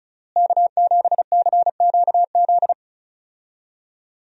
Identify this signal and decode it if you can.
Morse code, heard as a keyed tone, about 36 wpm, 700 Hz. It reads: K7CQZ
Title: USA Call Signs - 5 Characters - Repeated 3x - 35wpm